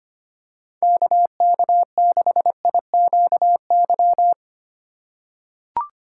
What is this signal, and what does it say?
Morse code, heard as a keyed tone, about 25 wpm, 700 Hz. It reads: KK6IQY
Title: USA Call Signs - 6 Characters - 4x Character Spacing - 25wpm